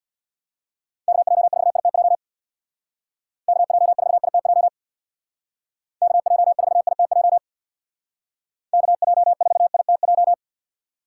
Morse code, heard as a keyed tone, about 65 wpm, 700 Hz. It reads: KJ4ITJ KJ4ITJ KJ4ITJ KJ4ITJ